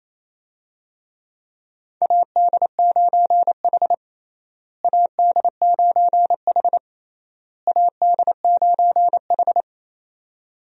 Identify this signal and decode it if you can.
Morse code, heard as a keyed tone, about 28 wpm, 700 Hz. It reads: AD9H AD9H AD9H
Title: USA Call Signs - 4 Characters - Repeated 3x - 28wpm